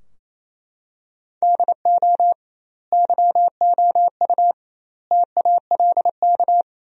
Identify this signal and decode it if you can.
Morse code, heard as a keyed tone, about 28 wpm, 700 Hz. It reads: DO YOU TALK
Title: 2x Word-Spacing - Sentences from Top 500 Words - 28wpm